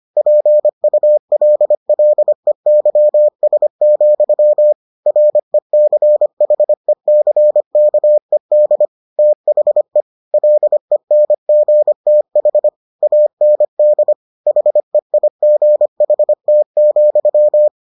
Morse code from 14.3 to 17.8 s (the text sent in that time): HEIGHT,